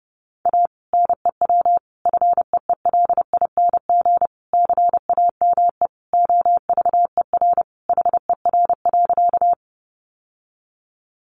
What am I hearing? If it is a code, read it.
Morse code, heard as a keyed tone, about 28 wpm, 700 Hz. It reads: A NEW FEELING CAME OVER HER.
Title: Sentences from Top 1200 Words 30wpm